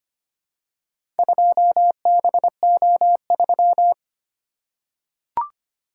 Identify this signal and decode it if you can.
Morse code, heard as a keyed tone, about 25 wpm, 700 Hz. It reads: 2BO3
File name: practice / 25wpm / Four Letter-Number Combinations - 25wpm